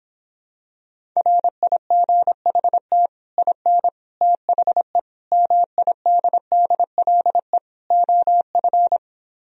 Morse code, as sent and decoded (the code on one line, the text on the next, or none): .-. .. --. .... - / .. -. / - .... / -- .. -.. -.. .-.. / --- ..-.
RIGHT IN THE MIDDLE OF